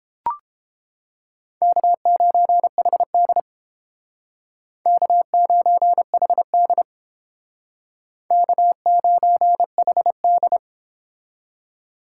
Morse code, heard as a keyed tone, about 29 wpm, 700 Hz. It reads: K9HD K9HD K9HD